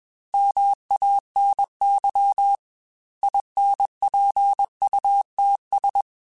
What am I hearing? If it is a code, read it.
Morse code, heard as a keyed tone, about 21 wpm, 800 Hz. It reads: MANY INPUTS
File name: Technical Subjects / Technical Subjects-mp3s / OR Gate @20